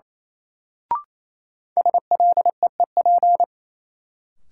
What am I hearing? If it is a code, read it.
Morse code, heard as a keyed tone, about 28 wpm, 700 Hz. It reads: SLEEP
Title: Sentences from Top 500 Words - New Word Drill - 28wpm